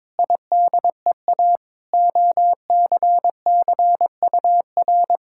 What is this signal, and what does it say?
Morse code, heard as a keyed tone, about 21 wpm, 700 Hz. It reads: IDEA OCCUR
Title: Sentences from Top 1000 Words - Review Part 6 - 22wpm